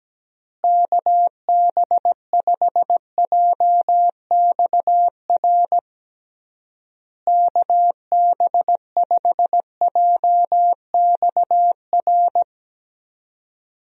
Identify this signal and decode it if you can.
Morse code, heard as a keyed tone, about 17 wpm, 700 Hz. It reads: KB5JXR KB5JXR
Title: USA Call Signs - 6 Characters - Repeated 3x - 17wpm